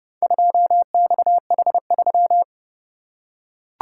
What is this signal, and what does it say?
Morse code, heard as a keyed tone, about 29 wpm, 700 Hz. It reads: 2XH3